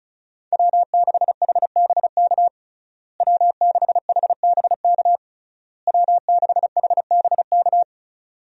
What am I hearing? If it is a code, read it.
Morse code, heard as a keyed tone, about 36 wpm, 700 Hz. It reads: W6HBK W6HBK W6HBK
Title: USA Call Signs - 5 Characters - Repeated 3x - 35wpm